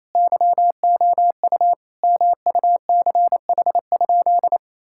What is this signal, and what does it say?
Morse code, heard as a keyed tone, about 28 wpm, 700 Hz. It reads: YOU MUCH?
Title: Sentences from Top 1000 Words - Review Part 6 - 28wpm